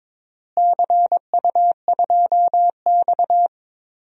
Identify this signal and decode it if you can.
Morse code, heard as a keyed tone, about 22 wpm, 700 Hz. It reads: CU2X
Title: DXCC-100 Call Signs - 4 Characters - Repeated 3x - 22wpm